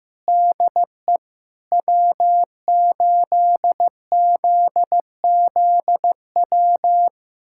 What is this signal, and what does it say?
Morse code, heard as a keyed tone, about 15 wpm, 700 Hz. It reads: DE W8ZZW
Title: USA Callsign DE Callsign 15wpm